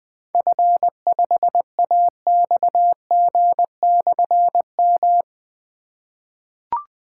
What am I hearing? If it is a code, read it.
Morse code, heard as a keyed tone, about 20 wpm, 700 Hz. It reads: F5AXG/M
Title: Difficult Call Signs 20wpm